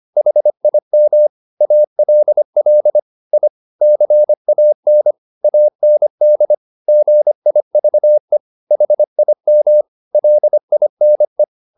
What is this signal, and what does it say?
Morse code, heard as a keyed tone, about 25 wpm, 600 Hz. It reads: HIM ALL I CAN AND GIVE HIM LINE
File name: the old man and the sea part 1 25wpm